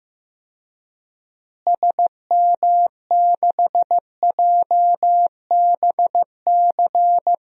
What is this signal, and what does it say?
Morse code, heard as a keyed tone, about 15 wpm, 700 Hz. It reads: SM6JBC
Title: DXCC-100 Call Signs - 6 Characters - Repeated 2x - 15wpm